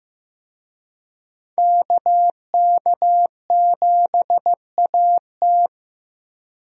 Morse code, heard as a keyed tone, about 16 wpm, 700 Hz. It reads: KK7AT
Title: USA Call Signs - 5 Characters - Repeated 3x - 15wpm